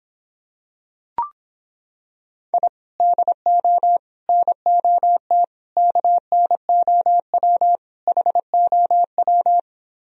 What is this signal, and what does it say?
Morse code, heard as a keyed tone, about 26 wpm, 700 Hz. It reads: I DO NOT KNOW HOW